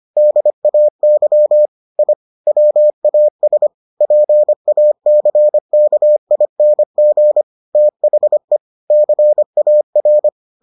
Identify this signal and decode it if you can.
Morse code, heard as a keyed tone, about 25 wpm, 600 Hz. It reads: DAY I WAS PACKING THE CAR